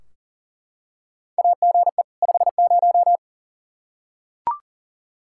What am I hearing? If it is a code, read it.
Morse code, heard as a keyed tone, about 40 wpm, 700 Hz. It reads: AGE 50
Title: Practice QSO - Exchange 3 - More Info - 40wpm